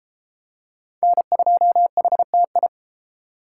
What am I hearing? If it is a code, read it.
Morse code, heard as a keyed tone, about 34 wpm, 700 Hz. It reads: N2HTI